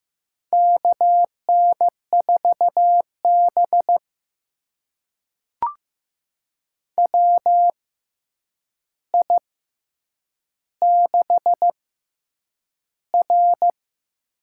Morse code, heard as a keyed tone, about 15 wpm, spaced as for 8 wpm, 700 Hz. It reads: KN4B WI6R
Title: USA Call Signs - 4 Characters - 6x Character Spacing - 15wpm